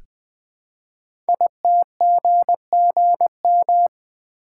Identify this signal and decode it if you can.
Morse code, heard as a keyed tone, about 20 wpm, 700 Hz. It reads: ITGGM
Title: Code Groups - 5 Letters - 20wpm